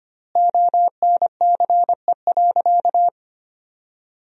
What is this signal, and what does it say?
Morse code, heard as a keyed tone, about 25 wpm, 700 Hz. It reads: ONCE.
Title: 3x Word-Spacing - Sentences from Top 500 Words - 25wpm